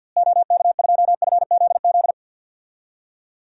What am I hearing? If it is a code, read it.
Morse code, heard as a keyed tone, about 47 wpm, 700 Hz. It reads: OK2FZZ
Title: DXCC-100 Call Signs - 6 Characters - Repeated 3x - 50wpm